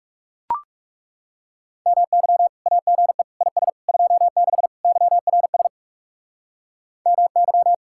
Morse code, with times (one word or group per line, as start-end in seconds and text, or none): 1.9-2.5 MY
2.7-3.2 AGE
3.4-3.7 IS
3.9-4.7 26
4.8-5.7 YRS
7.1-7.7 MY